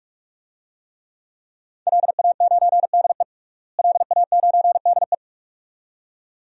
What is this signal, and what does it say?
Morse code, heard as a keyed tone, about 42 wpm, 700 Hz. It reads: LA9DE LA9DE
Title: DXCC-100 Call Signs - 5 Characters - Repeated 2x - 45wpm